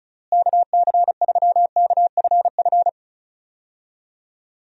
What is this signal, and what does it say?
Morse code, heard as a keyed tone, about 36 wpm, 700 Hz. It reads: KC3KFF
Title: USA Call Signs - 6 Characters - Repeated 2x - 35wpm